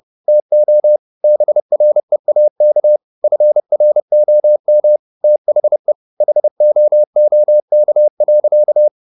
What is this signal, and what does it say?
Morse code, heard as a keyed tone, about 30 wpm, 600 Hz. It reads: TO BREAK FROM THE HOOK.
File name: the old man and the sea / the old man and the sea 1 30wpm